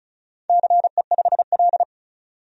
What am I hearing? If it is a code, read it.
Morse code, heard as a keyed tone, about 36 wpm, 700 Hz. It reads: CE5L
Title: DXCC-100 Call Signs - 4 Characters - 35wpm